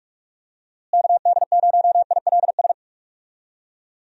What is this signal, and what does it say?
Morse code, heard as a keyed tone, about 46 wpm, 700 Hz. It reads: KD0ILS
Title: USA Call Signs - 6 Characters - Repeated 3x - 45wpm